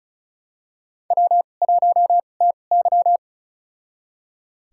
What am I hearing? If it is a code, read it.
Morse code, heard as a keyed tone, about 36 wpm, spaced as for 25 wpm, 700 Hz. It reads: W1TY
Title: USA Call Signs - 4 Characters - 2x Character Spacing - 35wpm